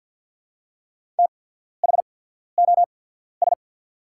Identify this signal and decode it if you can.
Morse code, heard as a keyed tone, about 50 wpm, spaced as for 14 wpm, 700 Hz. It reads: THOS
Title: 8x Character Spacing - Top 500 Words - 50wpm